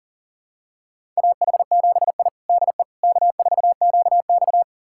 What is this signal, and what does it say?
Morse code, heard as a keyed tone, about 40 wpm, 700 Hz. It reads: AH7I DE K4QX
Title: USA Callsign DE Callsign - 4 Characters - 40wpm